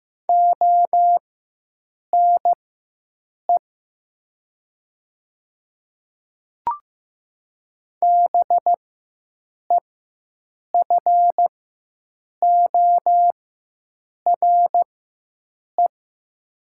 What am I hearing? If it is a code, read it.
Morse code, heard as a keyed tone, about 15 wpm, spaced as for 7 wpm, 700 Hz. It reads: ONE BEFORE
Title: 4x Character Spacing - Top 200 Words - 15wpm